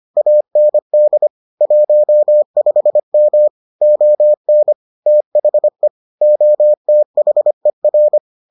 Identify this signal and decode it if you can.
Morse code, heard as a keyed tone, about 25 wpm, 600 Hz. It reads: AND 15M ON THE OTHER